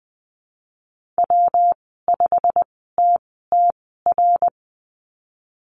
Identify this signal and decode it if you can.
Morse code, heard as a keyed tone, about 20 wpm, spaced as for 15 wpm, 700 Hz. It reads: W5TTR
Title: 179 Lesson 29g - Slash - Call Signs 20wpm